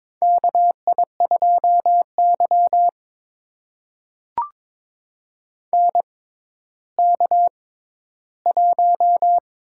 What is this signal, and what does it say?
Morse code, heard as a keyed tone, about 22 wpm, spaced as for 13 wpm, 700 Hz. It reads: KI2Y NK1
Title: USA Call Signs - 4 Characters - 6x Character Spacing - 22wpm